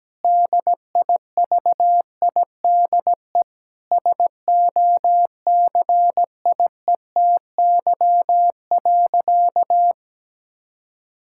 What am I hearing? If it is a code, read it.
Morse code, heard as a keyed tone, about 17 wpm, 700 Hz. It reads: DIVIDE SOCIETY.
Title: Sentences from Top 1600 Words 17wpm